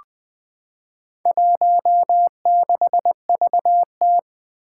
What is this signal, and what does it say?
Morse code, heard as a keyed tone, about 20 wpm, 700 Hz. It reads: 16VT
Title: Four Challenging Letter-Number Combinations 20wpm